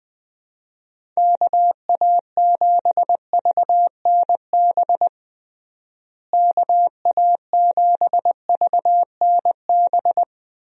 Morse code, heard as a keyed tone, about 20 wpm, 700 Hz. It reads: KA7VNB KA7VNB
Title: USA Call Signs - 6 Characters - Repeated 3x - 20wpm